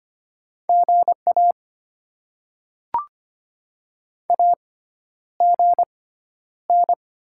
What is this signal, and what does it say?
Morse code, heard as a keyed tone, about 25 wpm, spaced as for 11 wpm, 700 Hz. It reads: GA AGN
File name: Top 20 QSO Elements - 6x Character Spacing - 25wpm